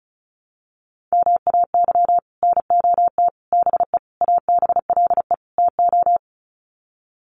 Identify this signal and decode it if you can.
Morse code, heard as a keyed tone, about 34 wpm, 700 Hz. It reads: MAY NOT BE ABLE TO
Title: Sets of 5 Words 35wpm